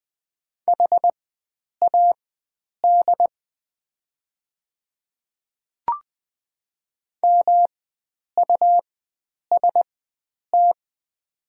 Morse code, heard as a keyed tone, about 20 wpm, spaced as for 9 wpm, 700 Hz. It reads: HAD MUST
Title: 4x Character Spacing - Top 500 Words - 20wpm